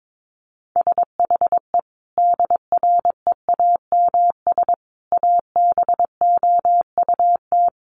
Morse code, heard as a keyed tone, about 22 wpm, 700 Hz. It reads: SHE DREAMS ABOUT
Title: Easy Phrases 22wpm